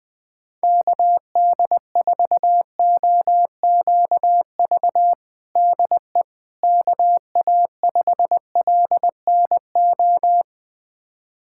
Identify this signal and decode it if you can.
Morse code, heard as a keyed tone, about 21 wpm, 700 Hz. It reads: KD4OQV DE KA5LNO
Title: USA Callsign DE Callsign - 6 Characters - 20wpm